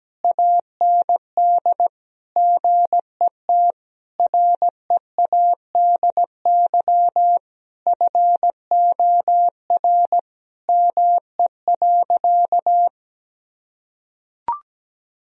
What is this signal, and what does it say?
Morse code, heard as a keyed tone, about 17 wpm, 700 Hz. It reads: AND GET READY FOR ME.
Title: Sentences from Top 1000 Words - Review Part 1 - 17wpm